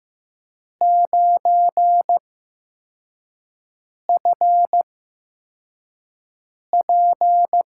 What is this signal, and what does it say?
Morse code, heard as a keyed tone, about 15 wpm, spaced as for 4 wpm, 700 Hz. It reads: 9FP
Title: USA Call Signs - 8x Character Spacing - 15wpm